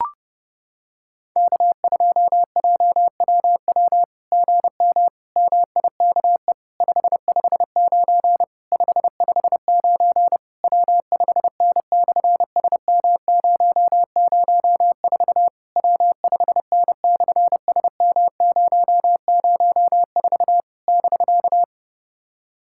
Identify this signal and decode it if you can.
Morse code, heard as a keyed tone, about 30 wpm, 700 Hz. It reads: K2JWW GM MIKE 559 559 W5N/SM004 W5N/SM004 <BK>
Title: SOTA QSO Practice 30wpm